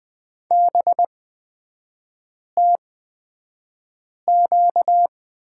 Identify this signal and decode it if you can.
Morse code, heard as a keyed tone, about 20 wpm, spaced as for 5 wpm, 700 Hz. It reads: BTQ
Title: 236 Lesson 38e - Letter q - Three Characters 20wpm